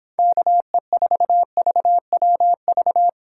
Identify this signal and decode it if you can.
Morse code, heard as a keyed tone, about 25 wpm, 700 Hz. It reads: KE4VWV
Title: Speed Racing - USA Call Signs 22wpm